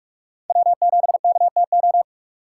Morse code, heard as a keyed tone, about 44 wpm, 700 Hz. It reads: W7KTO
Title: USA Call Signs - 5 Characters - 6x Character Spacing - 45wpm